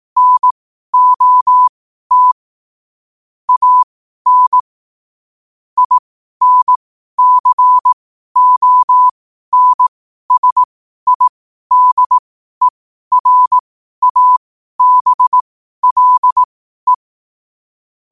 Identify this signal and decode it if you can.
Morse code, heard as a keyed tone, about 18 wpm, spaced as for 12 wpm, 1000 Hz. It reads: NOT AN INCONSIDERABLE
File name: qotdfiles / 12wpm